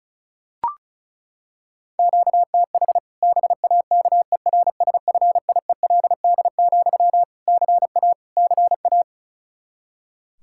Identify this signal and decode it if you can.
Morse code, heard as a keyed tone, about 34 wpm, 700 Hz. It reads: QTH BAKERSFIELD, CA CA
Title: Practice QSO - Exchange 1 - Beginning - 35wpm